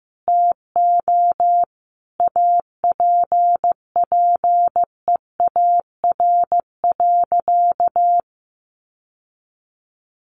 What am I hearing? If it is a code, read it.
Morse code, heard as a keyed tone, about 15 wpm, 700 Hz. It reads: TO APPEAR.
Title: Sentences from Top 1200 Words 15wpm